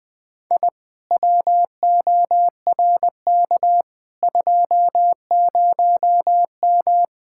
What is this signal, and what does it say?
Morse code, heard as a keyed tone, about 20 wpm, 700 Hz. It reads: I WORK 20M
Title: Speed-Racing Common QSO Phrases - 17wpm